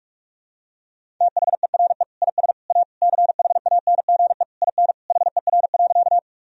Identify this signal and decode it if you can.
Morse code, heard as a keyed tone, about 46 wpm, 700 Hz. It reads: THERE IS A CHANGE IN HER.
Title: Sentences from Top 1000 Words - Review Part 1 - 45wpm